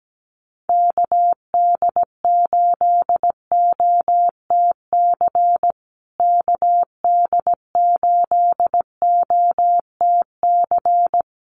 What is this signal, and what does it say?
Morse code, heard as a keyed tone, about 17 wpm, 700 Hz. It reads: KD8OTC KD8OTC
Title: USA Call Signs Repeated 3x 17wpm